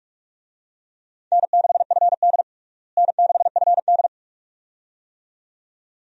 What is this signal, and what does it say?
Morse code, heard as a keyed tone, about 46 wpm, 700 Hz. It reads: N6FD N6FD
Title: USA Call Signs - 4 Characters - Repeated 2x - 45wpm